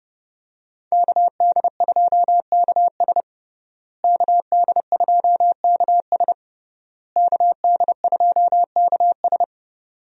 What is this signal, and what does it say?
Morse code, heard as a keyed tone, about 30 wpm, 700 Hz. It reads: KD2KS KD2KS KD2KS
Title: USA Call Signs - 5 Characters - Repeated 3x - 30wpm